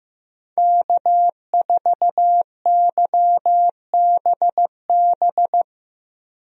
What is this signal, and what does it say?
Morse code, heard as a keyed tone, about 15 wpm, 700 Hz. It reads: K4YBB
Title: USA Call Signs - 5 Characters - Repeated 2x - 15wpm